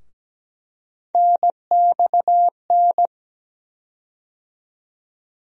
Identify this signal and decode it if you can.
Morse code, heard as a keyed tone, about 17 wpm, 700 Hz. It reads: NXN
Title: USA Call Sign Postfixes - 3 Letters 17wpm